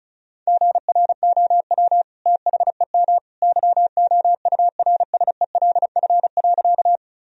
Morse code, heard as a keyed tone, about 34 wpm, 700 Hz. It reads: GROW THEM YOURSELF.
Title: Sentences from Top 1000 Words - Review Part 1 - 35wpm